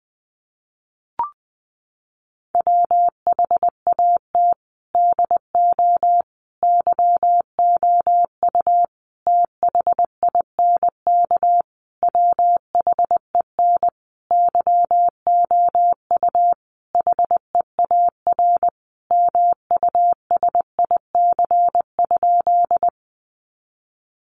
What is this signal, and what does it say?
Morse code, heard as a keyed tone, about 20 wpm, 700 Hz. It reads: WHAT DO YOU THINK WHEN YOU HEAR MUSIC?
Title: Sentences from Top 300 Words 20wpm